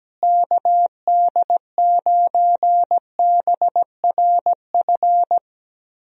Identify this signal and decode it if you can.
Morse code, heard as a keyed tone, about 17 wpm, 700 Hz. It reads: KD9BRF
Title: USA Call Signs - 6 Characters - Repeated 2x - 17wpm